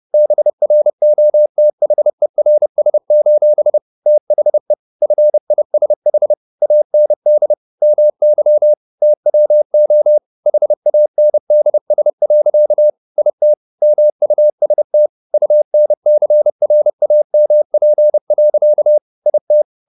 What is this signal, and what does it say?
Morse code, heard as a keyed tone, about 30 wpm, 600 Hz. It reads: BROTHERS: THE FISH AND MY TWO HANDS. IT MUST UNCRAMP. IT
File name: the old man and the sea / the old man and the sea 1 30wpm